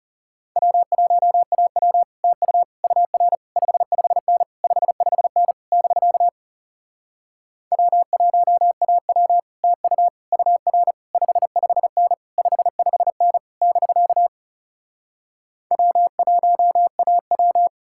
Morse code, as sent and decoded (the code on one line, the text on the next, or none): .-- .---- .- .-- / - ..- / ..- .-. / ..... ..... -. / ..... ..... -. / -...-.- / .-- .---- .- .-- / - ..- / ..- .-. / ..... ..... -. / ..... ..... -. / -...-.- / .-- .---- .- .--
W1AW TU UR 55N 55N <BK> W1AW TU UR 55N 55N <BK> W1AW